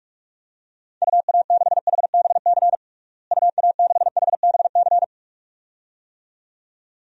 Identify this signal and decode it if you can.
Morse code, heard as a keyed tone, about 42 wpm, 700 Hz. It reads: UA6HBC UA6HBC
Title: DXCC-100 Call Signs - 6 Characters - Repeated 2x - 45wpm